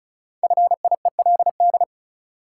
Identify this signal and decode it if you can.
Morse code, heard as a keyed tone, about 36 wpm, 700 Hz. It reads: FIELD
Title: Sentences from Top 1000 Words - New Word Drill - 35wpm